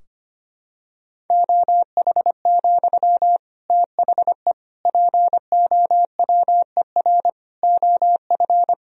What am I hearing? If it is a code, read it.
Morse code, heard as a keyed tone, about 25 wpm, 700 Hz. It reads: OH, THE POWER OF
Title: Sentences from Top 1000 Words - Review Part 3 - 25wpm